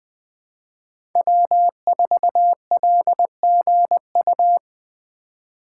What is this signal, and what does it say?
Morse code, heard as a keyed tone, about 20 wpm, 700 Hz. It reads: W4LGU
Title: Code Groups - 5 Letter:Numbers - 20wpm